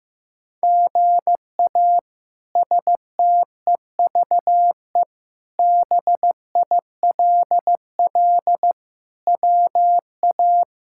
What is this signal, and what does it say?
Morse code, heard as a keyed tone, about 16 wpm, 700 Hz. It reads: GA STEVE BILL WA
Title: K1USN SST QSO Practice - 15wpm